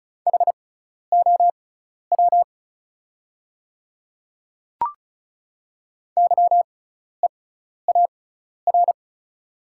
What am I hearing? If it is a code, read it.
Morse code, heard as a keyed tone, about 34 wpm, spaced as for 12 wpm, 690 Hz. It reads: HOW YEAR
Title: 6x Character Spacing - Top 200 Words - 35wpm